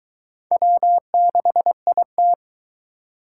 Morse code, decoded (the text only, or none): W6IT